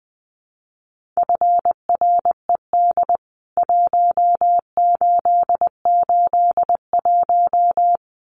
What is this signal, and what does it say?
Morse code, heard as a keyed tone, about 20 wpm, 700 Hz. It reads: FRED 1881
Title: CWT Exchange 20wpm